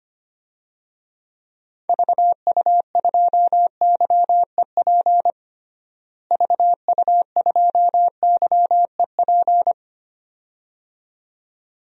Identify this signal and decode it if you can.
Morse code, heard as a keyed tone, about 25 wpm, 700 Hz. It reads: VU2YEP VU2YEP
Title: DXCC-100 Call Signs - 6 Characters - Repeated 2x - 25wpm